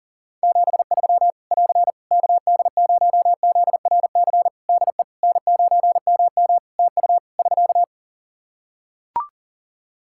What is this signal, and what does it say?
Morse code, heard as a keyed tone, about 40 wpm, 700 Hz. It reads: 73 <AR> KD0ZRC DE N9MM TU <SK>
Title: Practice QSO - Exchange 4 - Ending - 40wpm